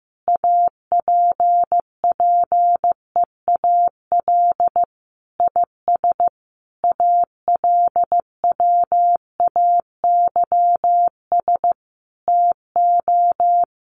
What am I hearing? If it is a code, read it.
Morse code, heard as a keyed tone, about 15 wpm, 700 Hz. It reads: APPEAL IS ALWAYS TO
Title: Sentences from Top 1300 Words 15wpm